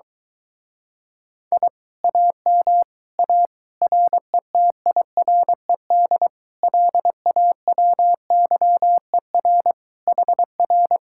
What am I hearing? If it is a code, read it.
Morse code, heard as a keyed tone, about 23 wpm, 700 Hz. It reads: I AM A RETIRED LAWYER HR